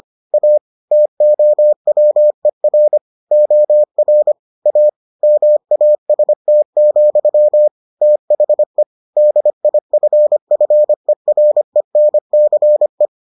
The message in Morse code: A TOWER OR A MAST, THE DIFFERENCE

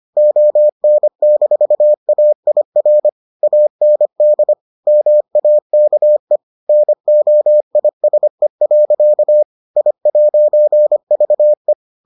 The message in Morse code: ON-AIR AND MAKE NOISE. I'VE

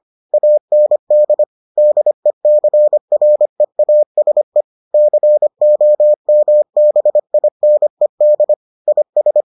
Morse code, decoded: AND DECREASE COMBINED IS

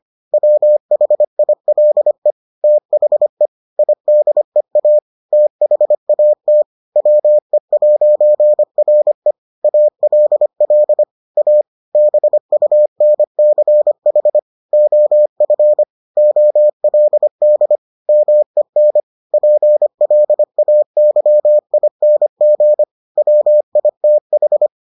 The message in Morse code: WHILE THE IDEA THAT WE'RE ALL A BUNCH OF OLD MEN PLAYING WITH

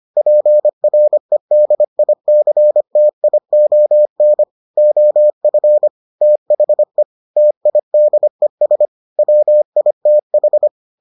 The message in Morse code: PREDICTION OF THE TIDES WITH

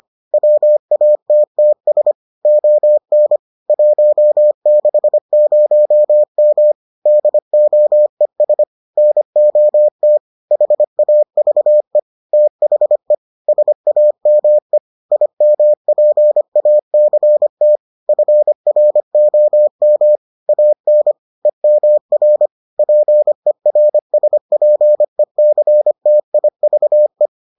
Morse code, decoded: WATTS ON 160M DOES NOT HAVE THE SAME IMPACT FROM AN EMR PERSPECTIVE

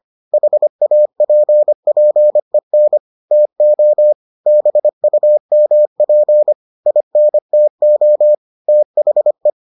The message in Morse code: HAPPEN TO BUMP INTO THE

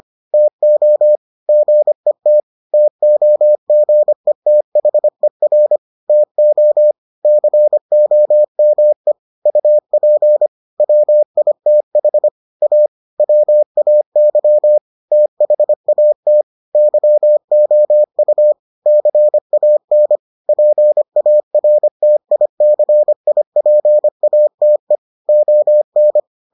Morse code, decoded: TO GET TOGETHER TO COME UP WITH A WAY THAT YOU CAN PARTICIPATE ON

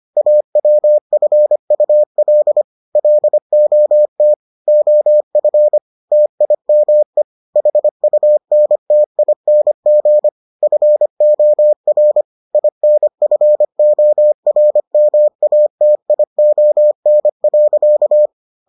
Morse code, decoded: AWFUL LOT OF TIME HUNTING FOR INFORMATION.

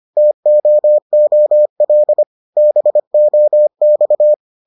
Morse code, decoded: TOOL BOX

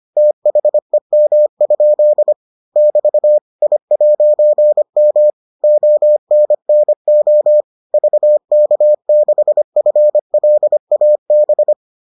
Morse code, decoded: THEM? <BT> I'M ONNO VK6FLAB

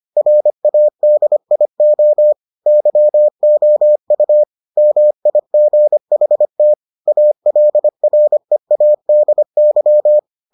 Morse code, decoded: RADIO YOU MIGHT ALREADY

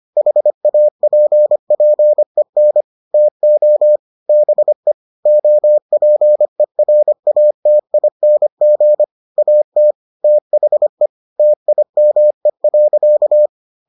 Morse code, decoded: HAPPEN TO BE OPERATING AT THE TIME.

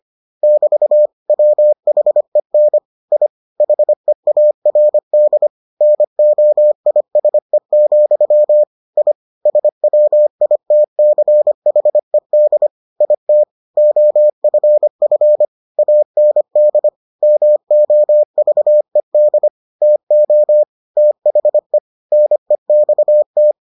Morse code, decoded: <BT> WHEN I HEARD NOISE, I SWITCHED IT OFF AND MOVED TO THE NEXT

